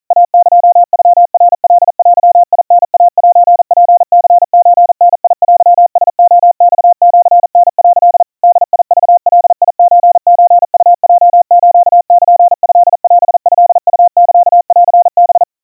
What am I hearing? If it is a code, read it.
Morse code (Japanese Wabun code), heard as a keyed tone, about 40 wpm, 700 Hz. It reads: イエノナカデタイセツニソダテラレマシタ」 ボクガソソウヲスルトオトウサンハ
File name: kawa / Morse sample / kamyu 150 cpm 700 Hz Crispy